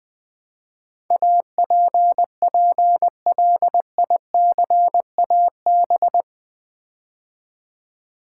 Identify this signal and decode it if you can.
Morse code, heard as a keyed tone, about 20 wpm, 700 Hz. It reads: APPLICAB
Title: Build A Word - 4 Letters At A Time - 10-Letter Words - 20wpm